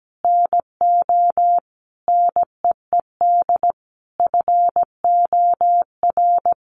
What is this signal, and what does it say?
Morse code, heard as a keyed tone, about 17 wpm, 700 Hz. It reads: NO NEED FOR
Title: Sentences from Top 1300 Words 17wpm